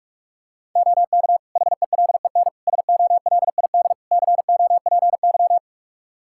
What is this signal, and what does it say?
Morse code, heard as a keyed tone, about 46 wpm, 700 Hz. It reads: OK HELEN SOLID COPY
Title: Practice QSO - Exchange 3 - More Info - 45wpm